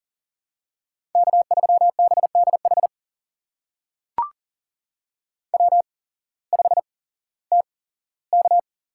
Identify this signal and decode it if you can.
Morse code, heard as a keyed tone, about 40 wpm, spaced as for 19 wpm, 700 Hz. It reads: K3BDH W5TK
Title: USA Call Signs - 5 Characters - 8x Character Spacing - 40wpm